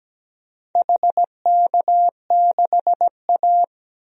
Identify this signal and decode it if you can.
Morse code, heard as a keyed tone, about 17 wpm, 700 Hz. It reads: HK6A
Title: DXCC-100 Call Signs - 4 Characters - 17wpm